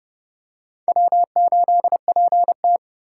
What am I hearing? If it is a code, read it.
Morse code, heard as a keyed tone, about 32 wpm, 700 Hz. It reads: W8PT
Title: USA Call Signs - 4x Character Spacing - 30wpm